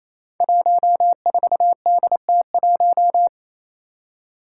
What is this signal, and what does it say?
Morse code, heard as a keyed tone, about 28 wpm, 700 Hz. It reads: 14DT1
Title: Five Challenging Letter-Number Combinations 28wpm